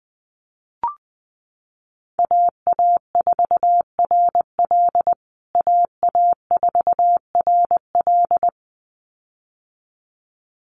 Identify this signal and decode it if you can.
Morse code, heard as a keyed tone, about 20 wpm, 700 Hz. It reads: AA4RL AA4RL
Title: USA Call Signs Repeated 2x 20wpm